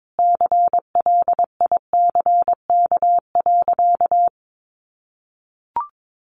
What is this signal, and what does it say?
Morse code, heard as a keyed tone, about 22 wpm, 700 Hz. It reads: CLICK.
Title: Sentences from Top 1100 Words 22wpm